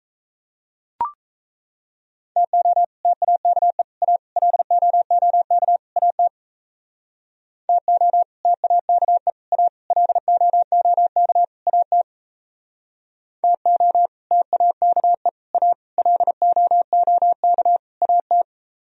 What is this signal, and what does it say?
Morse code, heard as a keyed tone, about 36 wpm, 700 Hz. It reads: TO TAKE A LOOK AT TO TAKE A LOOK AT TO TAKE A LOOK AT